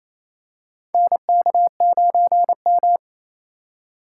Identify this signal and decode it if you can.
Morse code, heard as a keyed tone, about 28 wpm, 700 Hz. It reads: NK9M
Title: USA Call Signs - 4x Character Spacing - 28wpm